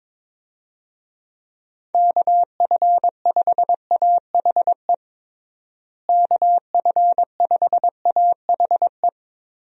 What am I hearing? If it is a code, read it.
Morse code, heard as a keyed tone, about 22 wpm, 700 Hz. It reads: KF5AHE KF5AHE
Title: USA Call Signs - 6 Characters - Repeated 2x - 22wpm